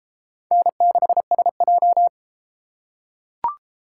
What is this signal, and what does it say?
Morse code, heard as a keyed tone, about 32 wpm, 700 Hz. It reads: N6SJ